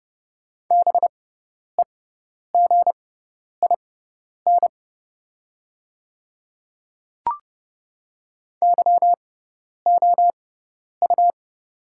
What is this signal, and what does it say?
Morse code, heard as a keyed tone, about 31 wpm, spaced as for 10 wpm, 700 Hz. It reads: BEGIN YOU
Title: 6x Character Spacing - Top 300 Words - 30wpm